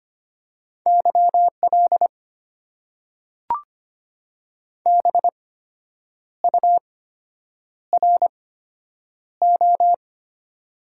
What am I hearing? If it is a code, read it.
Morse code, heard as a keyed tone, about 25 wpm, spaced as for 8 wpm, 690 Hz. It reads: YL BURO
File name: Top 90 QSO Elements - 8x Character Spacing - 25wpm